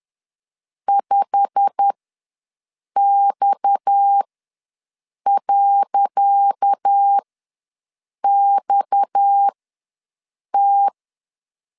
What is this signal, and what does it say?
Morse code, heard as a keyed tone, about 10 wpm, spaced as for 6 wpm, 800 Hz. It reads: 5X.XT
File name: LEZIONI / 10 SIGNS / 10C DOT ws 10wpm 4213